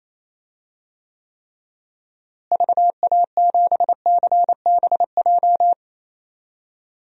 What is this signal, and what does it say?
Morse code, heard as a keyed tone, about 28 wpm, 700 Hz. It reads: VA7CBJ
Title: DXCC-100 Call Signs - 6 Characters - Repeated 3x - 28wpm